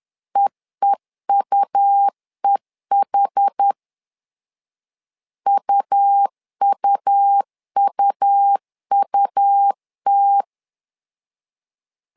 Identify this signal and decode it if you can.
Morse code, heard as a keyed tone, about 10 wpm, 800 Hz. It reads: EEUEH UUUUT
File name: LEZIONI / 2 UHDN / 2 UH 10wpm 1714